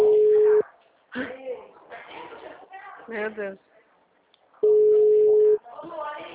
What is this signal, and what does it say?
Ringback tone (Europe, 425 Hz), with speech.
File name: out- -2003- - - .70078